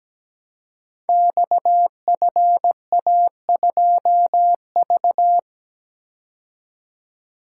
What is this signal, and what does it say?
Morse code, heard as a keyed tone, about 17 wpm, 700 Hz. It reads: XFA2V
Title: Code Groups - 5 Letter:Numbers - 17wpm